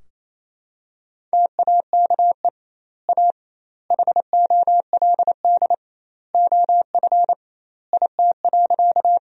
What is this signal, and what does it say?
Morse code, heard as a keyed tone, about 28 wpm, 700 Hz. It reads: TAKE A HOLD OF IT.
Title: 2x Word-Spacing - Sentences from Top 400 Words - 28wpm